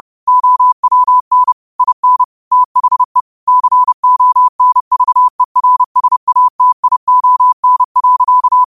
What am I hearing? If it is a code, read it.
Morse code, heard as a keyed tone, about 30 wpm, 1000 Hz. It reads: OWN IN THE CONVERSATION.